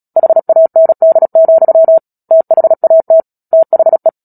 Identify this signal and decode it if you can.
Morse code, heard as a keyed tone, about 37 wpm, 650 Hz. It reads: HAND, THAT THE
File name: Book Two / 10-Chapter 10 - Epilogue - 36 WPM